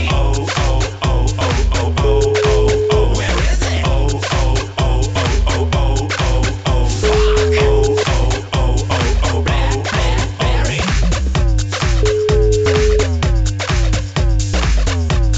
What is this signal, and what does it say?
Ringback tone (Europe, 425 Hz), with noise.